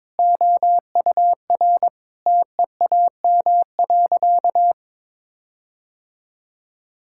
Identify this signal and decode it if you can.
Morse code, heard as a keyed tone, about 22 wpm, 700 Hz. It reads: OUR TEAM.
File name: practice / 22wpm / Sentences with 8-Word Phrases - 22wpm